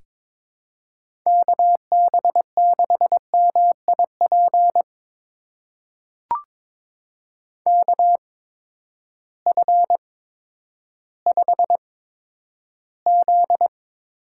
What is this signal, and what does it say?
Morse code, heard as a keyed tone, about 22 wpm, spaced as for 11 wpm, 700 Hz. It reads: KB6MIP KF5Z
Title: USA Call Signs - 8x Character Spacing - 22wpm